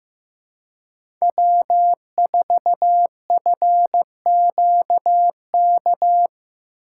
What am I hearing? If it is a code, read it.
Morse code, heard as a keyed tone, about 15 wpm, 700 Hz. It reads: W4FQK
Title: USA Call Signs - 6x Character Spacing - 15wpm